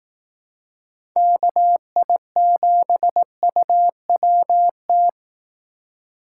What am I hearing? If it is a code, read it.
Morse code, heard as a keyed tone, about 18 wpm, 700 Hz. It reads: KI7UWT
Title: Speed Racing - USA 2x3 Call Signs 15wpm